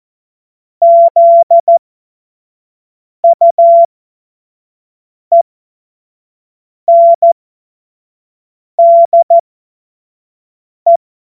Morse code, heard as a keyed tone, about 14 wpm, spaced as for 5 wpm, 680 Hz. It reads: ZUENDE
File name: Morsekurs / Witze / Witz 14. September 2021, 25 F70